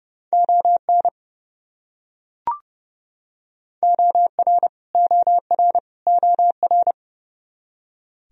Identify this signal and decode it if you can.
Morse code, heard as a keyed tone, about 30 wpm, 700 Hz. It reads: ON OR OR OR
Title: Top 100 Words Repeated 30wpm